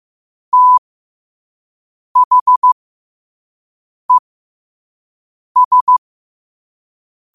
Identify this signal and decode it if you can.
Morse code, heard as a keyed tone, about 15 wpm, spaced as for 5 wpm, 1000 Hz. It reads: THES
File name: qotdfiles / 5wpm